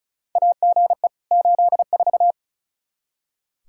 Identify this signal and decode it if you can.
Morse code, heard as a keyed tone, about 36 wpm, 700 Hz. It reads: AGE 84
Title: Practice QSO - Exchange 3 - More Info - 35wpm